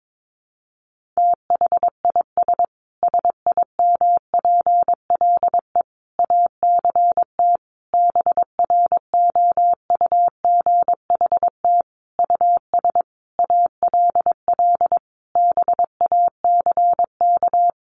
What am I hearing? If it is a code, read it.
Morse code, heard as a keyed tone, about 22 wpm, 700 Hz. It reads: THIS SIMPLE ACT BROUGHT US ALL BACK
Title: Sentences from Top 1500 Words 22wpm